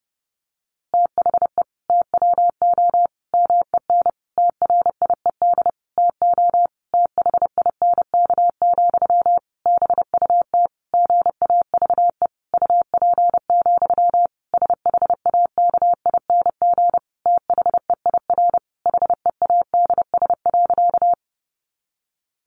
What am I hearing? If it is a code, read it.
Morse code, heard as a keyed tone, about 30 wpm, 700 Hz. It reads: THE TWO MEN TRIED TO THINK, BUT GAVE UP, SHAKING THEIR HEADS.
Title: A Sound Of Thunder 30wpm